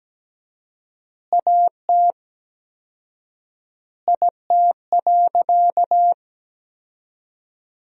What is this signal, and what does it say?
Morse code, heard as a keyed tone, about 17 wpm, 700 Hz. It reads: AT IT.
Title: 4x Word-Spacing - Sentences from Top 100 Words - 17wpm